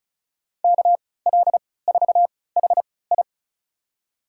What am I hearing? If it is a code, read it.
Morse code, heard as a keyed tone, about 34 wpm, spaced as for 20 wpm, 700 Hz. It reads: KL4HI
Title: USA Call Signs - 5 Characters - 3x Character Spacing - 35wpm